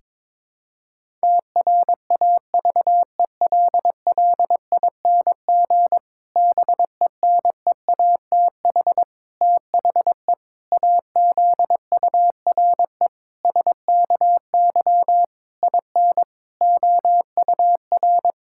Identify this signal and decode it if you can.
Morse code, heard as a keyed tone, about 22 wpm, 700 Hz. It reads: TRAVELLING BENEATH THE AZURE SKY IN OUR
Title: Pangram Challenge v2 - 22wpm